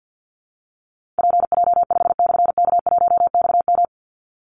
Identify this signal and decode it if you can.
Morse code, heard as a keyed tone, about 50 wpm, 700 Hz. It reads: PJ5/K1XM
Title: Call Signs Repeated 3x - 50wpm